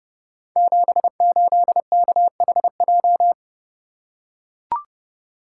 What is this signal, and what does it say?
Morse code, heard as a keyed tone, about 30 wpm, 700 Hz. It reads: 78KHJ
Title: Code Groups - 5 Letter:Numbers - 30wpm